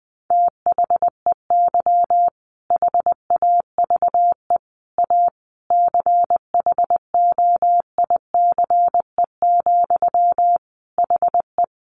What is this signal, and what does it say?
Morse code, heard as a keyed tone, about 20 wpm, 700 Hz. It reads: THEY HAVE A CHOICE, HE
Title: Sentences from Top 900 Words 20wpm